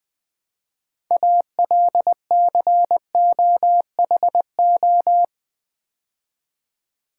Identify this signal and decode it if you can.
Morse code, heard as a keyed tone, about 20 wpm, 700 Hz. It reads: ALCOHO
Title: Build A Word - 3 Letters At A Time - 9-Letter Words - 20wpm